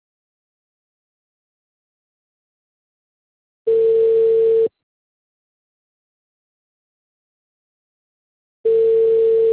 Ringback tone (China, 450 Hz).